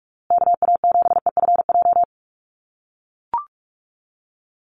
Morse code, heard as a keyed tone, about 44 wpm, 690 Hz. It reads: KA7EFJ
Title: USA Call Signs Repeated 2x 45wpm